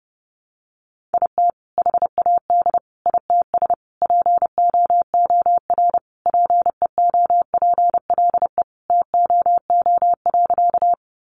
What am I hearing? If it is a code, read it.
Morse code, heard as a keyed tone, about 30 wpm, 700 Hz. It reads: IT HAD ITS POOR PEOPLE TOO.
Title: Sentences from Top 1000 Words 30wpm